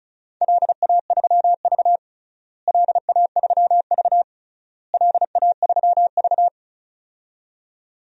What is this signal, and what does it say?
Morse code, heard as a keyed tone, about 34 wpm, 700 Hz. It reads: LA3V LA3V LA3V
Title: DXCC-100 Call Signs - 4 Characters - Repeated 3x - 35wpm